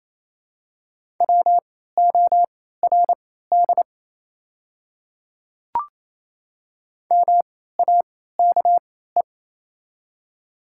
Morse code, heard as a keyed tone, about 28 wpm, spaced as for 16 wpm, 700 Hz. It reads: WORD MAKE
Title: 3x Character Spacing - Top 200 Words - 28wpm